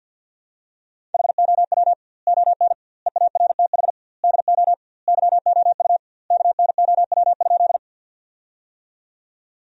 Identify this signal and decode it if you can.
Morse code, heard as a keyed tone, about 50 wpm, 700 Hz. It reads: HOW ON EARTH DO YOU KNOW?
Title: 2x Word-Spacing - Sentences from Top 200 Words - 50wpm